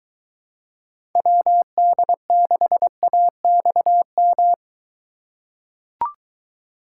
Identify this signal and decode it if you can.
Morse code, heard as a keyed tone, about 22 wpm, 700 Hz. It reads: WD6AXM